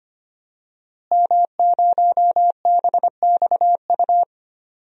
Morse code, heard as a keyed tone, about 25 wpm, 700 Hz. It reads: M0BXU
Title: DXCC-100 Call Signs - 5 Characters - Repeated 3x - 25wpm